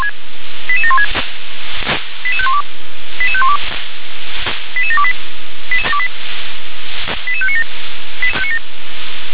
Five-tone selective call (ZVEI), heard as quick runs of five tones, tones of about 70 ms. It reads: 88716 88512 88512 88728 88597